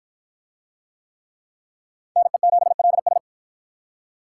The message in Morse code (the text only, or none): NE7LS